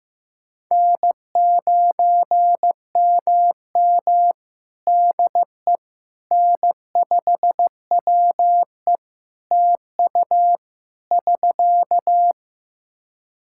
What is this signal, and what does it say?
Morse code, heard as a keyed tone, about 14 wpm, 700 Hz. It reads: N9MM DE N5WE TU <SK>
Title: Practice QSO - Exchange 4 - Ending - 15wpm